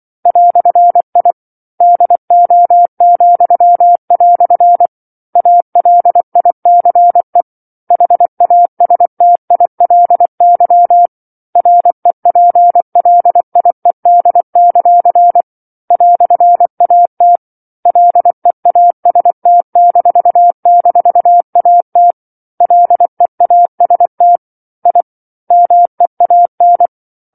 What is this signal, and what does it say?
Morse code, heard as a keyed tone, about 24 wpm, 700 Hz. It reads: "I DO," ALICE HASTILY REPLIED; "AT LEAST--AT LEAST I MEAN
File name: 07-Chapter-07 24 WPM